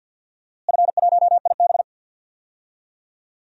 Morse code, heard as a keyed tone, about 52 wpm, 700 Hz. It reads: F1IB
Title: DXCC-100 Call Signs - 4 Characters - Repeated 3x - 50wpm